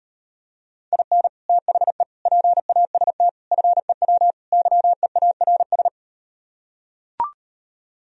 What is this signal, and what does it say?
Morse code, heard as a keyed tone, about 38 wpm, 700 Hz. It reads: IN THE PAST FEW YEARS